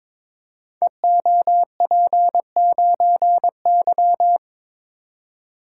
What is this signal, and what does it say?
Morse code, heard as a keyed tone, about 23 wpm, 690 Hz. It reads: EOP9Y
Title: Code Groups - 5 Letter:Numbers - 22wpm